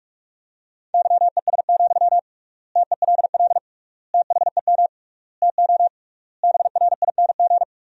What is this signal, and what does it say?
Morse code, heard as a keyed tone, about 46 wpm, 700 Hz. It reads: YES, TELL THEM TO BRING
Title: 3x Word-Spacing - Sentences from Top 500 Words - 45wpm